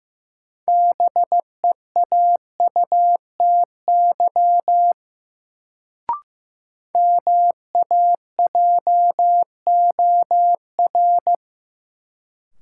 Morse code, heard as a keyed tone, about 15 wpm, 700 Hz. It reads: BEAUTY MAJOR
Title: Sentences from Top 600 Words - New Word Drill - 15wpm